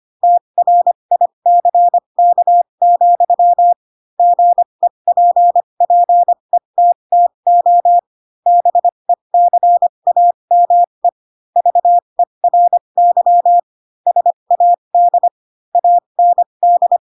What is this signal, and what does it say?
Morse code, heard as a keyed tone, about 25 wpm, spaced as for 22 wpm, 700 Hz. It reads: TRICK, GEPPETTO BECAME VERY SAD AND
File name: Pinocchio / Pinocchio ch3 18 25 ik1hge